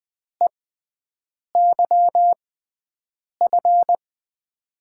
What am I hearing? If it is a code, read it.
Morse code, heard as a keyed tone, about 20 wpm, spaced as for 7 wpm, 700 Hz. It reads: EYF
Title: USA Call Signs - 6 Characters - 6x Character Spacing - 20wpm